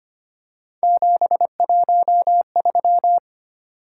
Morse code, heard as a keyed tone, about 25 wpm, 700 Hz. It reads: 713
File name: Three Letter-Number Combinations - 25wpm